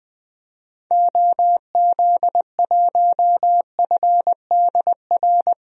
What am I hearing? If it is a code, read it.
Morse code, heard as a keyed tone, about 20 wpm, 700 Hz. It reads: OZ1FDR
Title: DXCC-100 Call Signs - 6 Characters - Repeated 2x - 20wpm